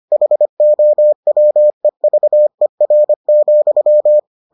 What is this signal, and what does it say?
Morse code, heard as a keyed tone, about 25 wpm, 600 Hz. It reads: HOWEVER,